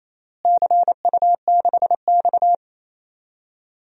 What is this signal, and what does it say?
Morse code, heard as a keyed tone, about 28 wpm, 700 Hz. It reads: CU6X
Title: DXCC-100 Call Signs - 4 Characters - Repeated 3x - 28wpm